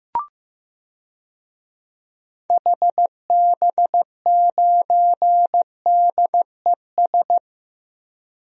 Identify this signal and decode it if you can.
Morse code, heard as a keyed tone, about 15 wpm, 700 Hz. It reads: HB9DES
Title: DXCC-100 Call Signs - 6 Characters - Repeated 2x - 15wpm